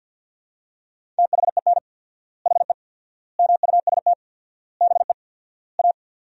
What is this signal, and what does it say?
Morse code, heard as a keyed tone, about 53 wpm, 700 Hz. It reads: THEN HE MUST BE A
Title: 4x Word-Spacing - Sentences from Top 500 Words - 50wpm